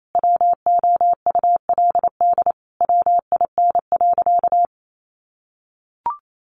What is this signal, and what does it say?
Morse code, heard as a keyed tone, about 28 wpm, 700 Hz. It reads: WOULD WIN.
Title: Sentences from Top 1500 Words 28wpm